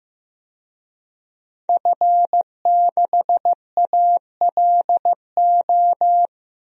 Morse code, heard as a keyed tone, about 15 wpm, 700 Hz. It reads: F6ALO
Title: DXCC-100 Call Signs - 5 Characters - Repeated 3x - 15wpm